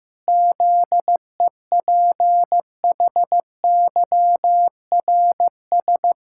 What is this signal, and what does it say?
Morse code, heard as a keyed tone, about 15 wpm, 700 Hz. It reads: ZEPHYRS
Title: Pangram Challenge v2 - 15wpm